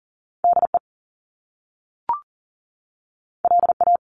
Morse code, heard as a keyed tone, about 40 wpm, 700 Hz. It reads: DE LA
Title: Top Two Letter Combinations in Words 40wpm